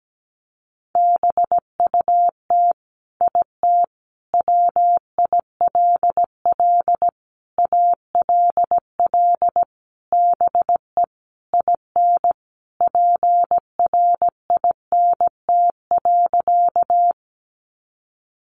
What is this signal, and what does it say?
Morse code, heard as a keyed tone, about 17 wpm, 700 Hz. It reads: BUT IT WILL ALL BE IN PRINT.
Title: Sentences from Top 1400 Words 17wpm